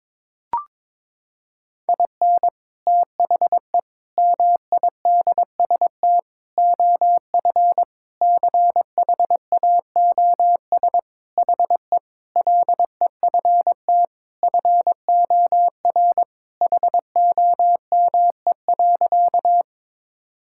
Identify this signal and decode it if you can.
Morse code, heard as a keyed tone, about 22 wpm, 700 Hz. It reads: IN THE MIDST OF CHAOS HE LEFT FOR HOME.
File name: practice / 22wpm / Sentences with 4-Word Phrases - 22wpm